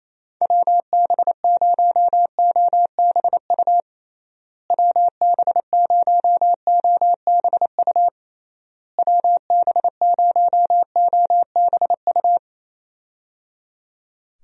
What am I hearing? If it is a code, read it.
Morse code, heard as a keyed tone, about 28 wpm, 700 Hz. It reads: WB0OBU WB0OBU WB0OBU
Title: USA Call Signs - 6 Characters - Repeated 3x - 28wpm